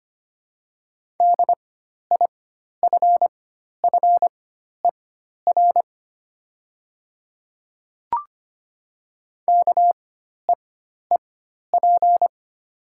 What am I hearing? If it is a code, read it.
Morse code, heard as a keyed tone, about 25 wpm, spaced as for 12 wpm, 700 Hz. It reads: DIFFER KEEP
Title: 4x Character Spacing - Top 500 Words - 25wpm